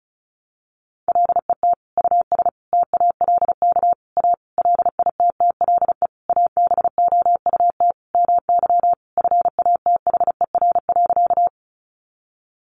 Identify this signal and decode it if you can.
Morse code, heard as a keyed tone, about 34 wpm, 700 Hz. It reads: LET US TALK A LITTLE ABOUT MY FATHER.
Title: Sentences from Top 400 Words 35wpm